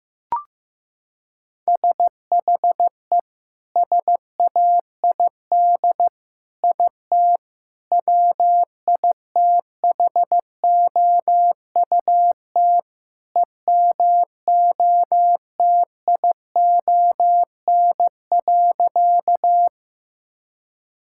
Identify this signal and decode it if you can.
Morse code, heard as a keyed tone, about 15 wpm, 700 Hz. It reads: SHE SAID IT WITHOUT EMOTION.
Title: Sentences from Top 1600 Words 15wpm